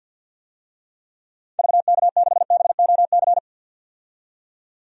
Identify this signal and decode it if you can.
Morse code, heard as a keyed tone, about 52 wpm, 700 Hz. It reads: VK6BOC
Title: DXCC-100 Call Signs - 6 Characters - 50wpm